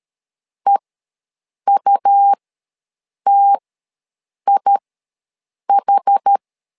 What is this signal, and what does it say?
Morse code, heard as a keyed tone, about 13 wpm, spaced as for 7 wpm, 800 Hz. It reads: EUTIH
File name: LEZIONI / 2 UHDN / 2 UH ws 12wpm 1730